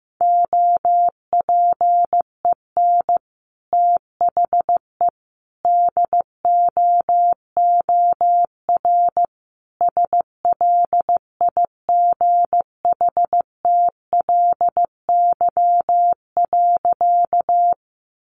Morse code, heard as a keyed tone, about 15 wpm, 700 Hz. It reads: OPEN THE DOOR SLIGHTLY.
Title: Sentences from Top 1300 Words 15wpm